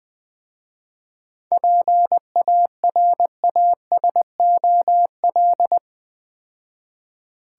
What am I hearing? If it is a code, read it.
Morse code, heard as a keyed tone, about 22 wpm, 700 Hz. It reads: PARASOL
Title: Easy Words - Speed-Racing with Hints - 20wpm